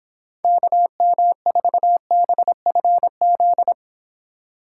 Morse code, heard as a keyed tone, about 26 wpm, 700 Hz. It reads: KM4BFZ